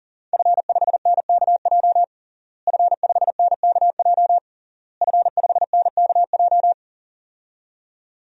Morse code, heard as a keyed tone, about 40 wpm, 700 Hz. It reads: F5NKJ F5NKJ F5NKJ
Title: DXCC-100 Call Signs - 5 Characters - Repeated 3x - 40wpm